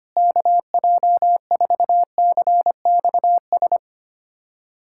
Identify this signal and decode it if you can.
Morse code, heard as a keyed tone, about 25 wpm, 700 Hz. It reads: KJ4CXS
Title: USA Callsign DE Callsign - 6 Characters - 25wpm